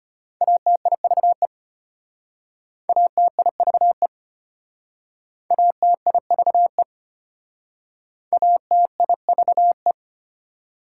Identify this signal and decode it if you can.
Morse code, heard as a keyed tone, about 34 wpm, 700 Hz. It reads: ATIVE ATIVE ATIVE ATIVE